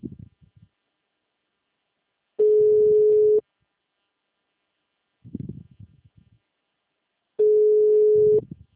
Ringback tone (Europe, 425 Hz).
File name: out- -2011- - - .18224